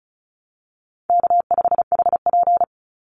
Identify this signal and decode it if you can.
Morse code, heard as a keyed tone, about 34 wpm, 700 Hz. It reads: K5HP
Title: Easy Call Signs Repeated 3x - 35wpm